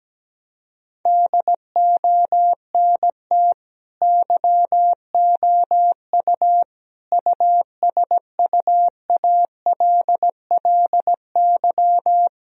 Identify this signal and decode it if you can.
Morse code, heard as a keyed tone, about 17 wpm, 700 Hz. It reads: DONT YOU USUALLY
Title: Sentences from Top 1000 Words - Review Part 3 - 17wpm